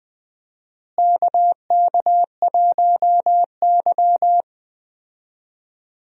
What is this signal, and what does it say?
Morse code, heard as a keyed tone, about 20 wpm, 700 Hz. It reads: KK1Y
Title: USA Call Signs - 4 Characters - Repeated 2x - 20wpm